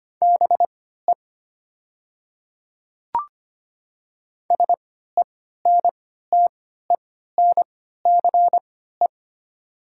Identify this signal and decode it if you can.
Morse code, heard as a keyed tone, about 25 wpm, spaced as for 14 wpm, 700 Hz. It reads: BE SENTENCE
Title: 3x Character Spacing - Top 200 Words - 25wpm